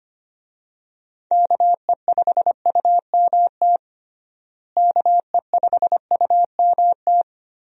Morse code, heard as a keyed tone, about 25 wpm, 700 Hz. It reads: KE5UMT KE5UMT
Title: USA Call Signs - 6 Characters - Repeated 2x - 25wpm